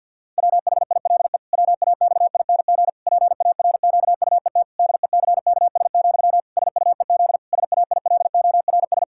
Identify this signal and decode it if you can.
Morse code, heard as a keyed tone, about 48 wpm, 700 Hz. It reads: WHILE WAXING PARQUET DECKS, SUEZ SAILORS
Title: Pangram Challenge v2 - 50wpm